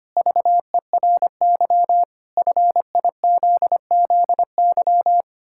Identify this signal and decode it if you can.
Morse code, heard as a keyed tone, about 25 wpm, 700 Hz. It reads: VERY FIZZY
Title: Pangram Challenge v2 - 25wpm